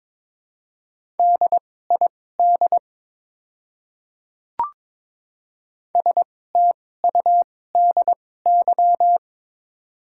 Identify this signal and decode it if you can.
Morse code, heard as a keyed tone, about 22 wpm, spaced as for 16 wpm, 700 Hz. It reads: DID STUDY
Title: 2x Character Spacing - Top 300 Words - 22wpm